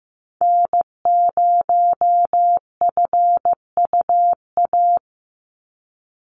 Morse code, heard as a keyed tone, about 15 wpm, 700 Hz. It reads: N0FUA
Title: USA Call Signs 15wpm